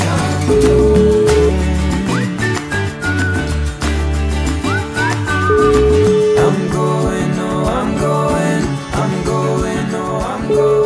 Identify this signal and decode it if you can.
Ringback tone (Europe, 425 Hz), with noise.